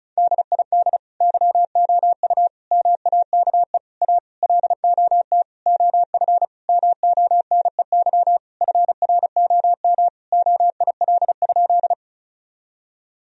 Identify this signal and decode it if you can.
Morse code, heard as a keyed tone, about 36 wpm, 700 Hz. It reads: DID YOU MAKE A LOT OF MONEY FROM OIL?
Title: Sentences from Top 1000 Words - Review Part 4 - 35wpm